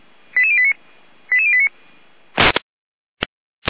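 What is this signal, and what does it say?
Five-tone selective call (ZVEI), heard as quick runs of five tones, tones of about 70 ms. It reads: 88089 88089
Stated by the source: ZVEI five-tone selective call 88082 received by SDR.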